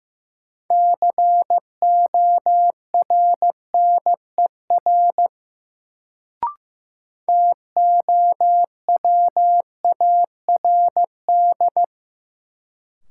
Morse code, heard as a keyed tone, about 16 wpm, 700 Hz. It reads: CORNER TOWARD
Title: Sets of 5 Words - New Word Drill - 15wpm